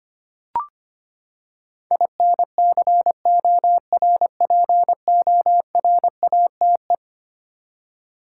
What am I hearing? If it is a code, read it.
Morse code, heard as a keyed tone, about 25 wpm, 700 Hz. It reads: INCORPORATE
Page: morse-code-ninja-practice-25wpm